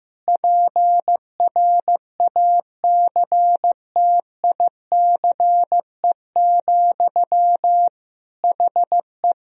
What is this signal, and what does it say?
Morse code, heard as a keyed tone, about 15 wpm, 700 Hz. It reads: PRACTICE, HE
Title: Sentences from Top 1000 Words - Review Part 4 - 15wpm